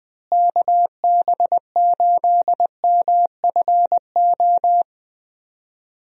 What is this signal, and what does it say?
Morse code, heard as a keyed tone, about 20 wpm, 700 Hz. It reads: KB8MFO
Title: USA Call Signs - 6 Characters - Repeated 2x - 20wpm